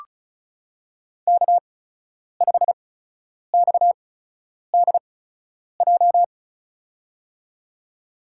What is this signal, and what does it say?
Morse code, heard as a keyed tone, about 36 wpm, spaced as for 10 wpm, 700 Hz. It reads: K5XDJ
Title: USA Call Signs - 8x Character Spacing - 35wpm